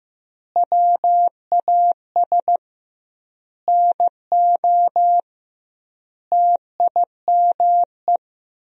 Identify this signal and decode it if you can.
Morse code, heard as a keyed tone, about 15 wpm, 700 Hz. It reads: WAS NO TIME
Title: 2x Word-Spacing - Sentences from Top 400 Words - 15wpm